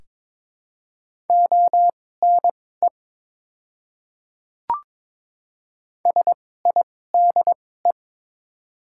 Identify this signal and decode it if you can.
Morse code, heard as a keyed tone, about 22 wpm, spaced as for 16 wpm, 700 Hz. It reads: ONE SIDE
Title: 2x Character Spacing - Top 100 Words - 22wpm